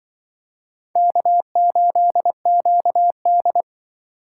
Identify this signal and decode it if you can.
Morse code, heard as a keyed tone, about 25 wpm, 700 Hz. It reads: K8QD
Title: Speed Racing - USA Call Signs 20wpm